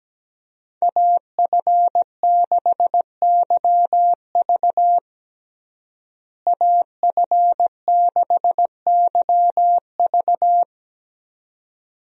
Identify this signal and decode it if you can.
Morse code, heard as a keyed tone, about 17 wpm, 700 Hz. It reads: AF6YV AF6YV
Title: USA Call Signs - 5 Characters - Repeated 2x - 17wpm